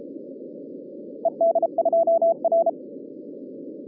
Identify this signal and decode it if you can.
Morse code, heard as a keyed tone, about 37 wpm, 660 Hz. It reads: ED2R
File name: 2011 bcc-challenge sonntag tape